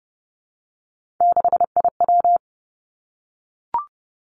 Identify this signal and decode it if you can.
Morse code, heard as a keyed tone, about 30 wpm, 700 Hz. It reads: 6IW
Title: Three Challenging Letter-Number Combinations 30wpm